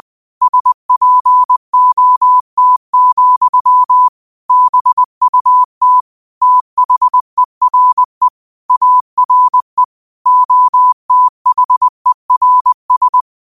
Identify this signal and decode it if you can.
Morse code, heard as a keyed tone, about 20 wpm, 1000 Hz. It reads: SPOT, BUT THERE ARE OTHERS